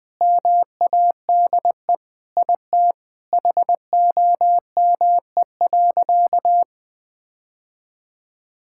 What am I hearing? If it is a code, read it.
Morse code, heard as a keyed tone, about 21 wpm, 700 Hz. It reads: MADE IT HOME.
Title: Sentences with 4-Word Phrases - 20wpm